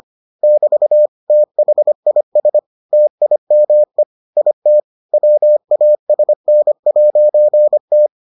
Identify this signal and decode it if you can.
Morse code, heard as a keyed tone, about 25 wpm, 600 Hz. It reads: <BT> THIS TIME IT WASN'T